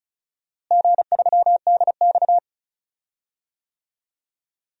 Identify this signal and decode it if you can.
Morse code, heard as a keyed tone, about 36 wpm, 700 Hz. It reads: G3DX
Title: DXCC-100 Call Signs - 4 Characters - Repeated 2x - 35wpm